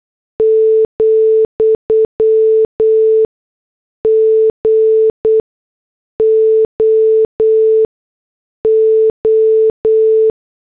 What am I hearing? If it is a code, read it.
Morse code, heard as a keyed tone, about 8 wpm, spaced as for 6 wpm, 440 Hz. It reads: , GOO